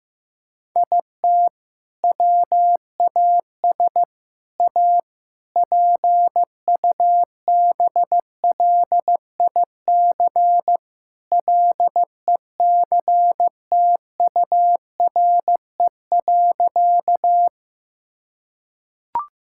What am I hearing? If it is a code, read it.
Morse code, heard as a keyed tone, about 15 wpm, 700 Hz. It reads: IT WAS A PUBLIC LECTURE.
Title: Sentences from Top 1600 Words 15wpm